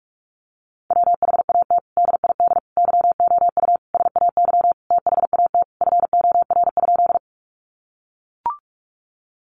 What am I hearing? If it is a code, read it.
Morse code, heard as a keyed tone, about 44 wpm, 700 Hz. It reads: WHAT DID YOU SAY THAT FOR?
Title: Sentences from Top 200 Words 45wpm